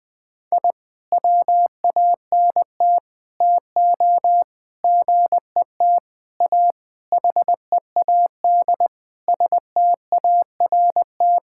Morse code, heard as a keyed tone, about 20 wpm, 700 Hz. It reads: I WANT TO GET A HEAD START
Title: Sentences with 7-Word Phrases - 20wpm